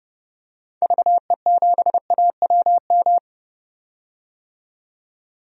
Morse code, heard as a keyed tone, about 30 wpm, 700 Hz. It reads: VE7AWM
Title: DXCC-100 Call Signs - 6 Characters - 30wpm